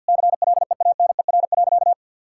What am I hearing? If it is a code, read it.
Morse code, heard as a keyed tone, about 50 wpm, 700 Hz. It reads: CLEANER.
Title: Pangram Challenge v2 - 50wpm